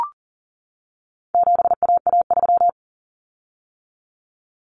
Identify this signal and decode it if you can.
Morse code, heard as a keyed tone, about 40 wpm, 700 Hz. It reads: 7AA3
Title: Common Cut Numbers - 4-Digit 40wpm